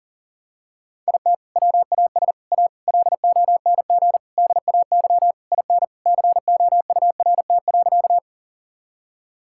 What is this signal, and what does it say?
Morse code, heard as a keyed tone, about 40 wpm, 700 Hz. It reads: IT WAS A LONG DAY IN COURT.
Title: Sentences from Top 1600 Words 40wpm